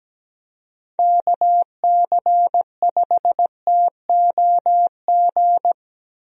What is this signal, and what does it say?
Morse code, heard as a keyed tone, about 17 wpm, 700 Hz. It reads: KC5TOG
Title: USA Call Signs - 6 Characters - Repeated 2x - 17wpm